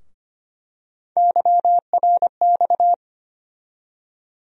Morse code, heard as a keyed tone, about 25 wpm, 700 Hz. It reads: YRX
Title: USA Call Sign Postfixes - 3 Letters 25wpm